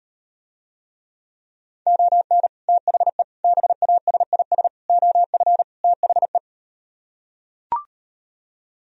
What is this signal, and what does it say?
Morse code, heard as a keyed tone, about 39 wpm, 700 Hz. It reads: ON THE BASIS OF THE